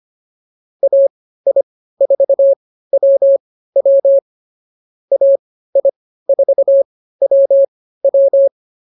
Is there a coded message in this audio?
Yes: Morse code from 0.8 to 8.5 s, heard as a keyed tone, about 24 wpm, spaced as for 15 wpm, 550 Hz. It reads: AI4WW AI4WW